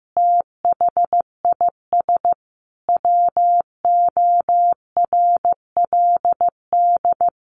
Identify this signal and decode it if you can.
Morse code, heard as a keyed tone, about 15 wpm, 700 Hz. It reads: THIS WORLD
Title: A Sound Of Thunder 15wpm